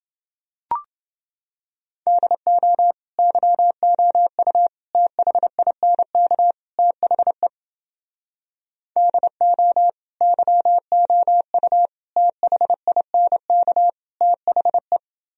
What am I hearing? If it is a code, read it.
Morse code, heard as a keyed tone, about 30 wpm, 700 Hz. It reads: DO YOU THINK THE DO YOU THINK THE